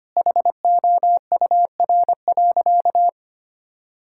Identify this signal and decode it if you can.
Morse code, heard as a keyed tone, about 25 wpm, 700 Hz. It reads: HOUR.
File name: practice / 25wpm / Sentences with 3-Word Phrases - 25wpm